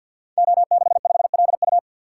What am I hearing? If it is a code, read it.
Morse code, heard as a keyed tone, about 47 wpm, 700 Hz. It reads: O65LU
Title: Code Groups - 5 Letter:Numbers - 50wpm